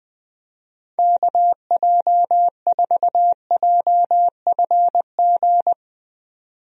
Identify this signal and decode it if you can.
Morse code, heard as a keyed tone, about 20 wpm, 700 Hz. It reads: KJ4JFG
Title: USA Call Signs - 6 Characters - 20wpm